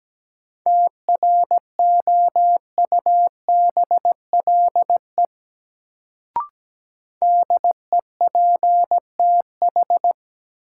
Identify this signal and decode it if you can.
Morse code, heard as a keyed tone, about 17 wpm, 700 Hz. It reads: TROUBLE DEPTH
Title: Sentences from Top 1200 Words - New Word Drill Alt - 17wpm